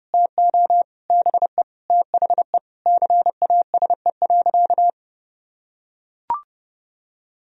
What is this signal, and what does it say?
Morse code, heard as a keyed tone, about 30 wpm, 700 Hz. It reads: TO BE THE CASE.
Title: Sentences from Top 1000 Words - Review Part 2 - 30wpm